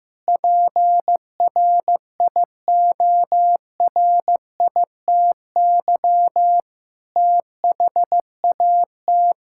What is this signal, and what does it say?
Morse code, heard as a keyed tone, about 15 wpm, 700 Hz. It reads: PRIORITY THAT
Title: Sentences from Top 1600 Words 15wpm